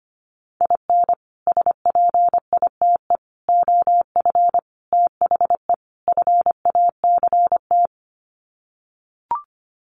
Morse code, heard as a keyed tone, about 25 wpm, 700 Hz. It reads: IN SPITE OF THE FACT
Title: Sets of 5 Words Encore2 25wpm